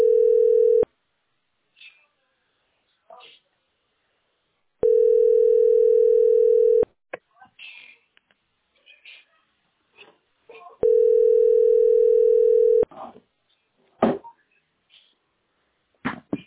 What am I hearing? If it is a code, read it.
Ringback tone (North America), 440 + 480 Hz, with speech.
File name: f98214b8-51ca-4fc1-966c